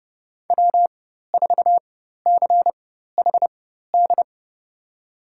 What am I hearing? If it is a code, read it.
Morse code, heard as a keyed tone, about 31 wpm, spaced as for 14 wpm, 700 Hz. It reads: W4CHD
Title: USA Call Signs - 5 Characters - 4x Character Spacing - 30wpm